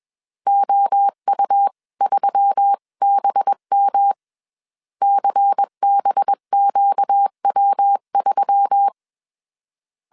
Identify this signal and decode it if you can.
Morse code, heard as a keyed tone, about 21 wpm, 800 Hz. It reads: OU3BM CBQW3
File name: CW / DET / 1 6 1500C S C 20wpm 2116